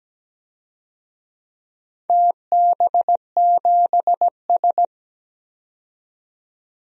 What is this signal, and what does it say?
Morse code, heard as a keyed tone, about 17 wpm, 700 Hz. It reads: TB7S
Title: DXCC-100 Call Signs - 4 Characters - 17wpm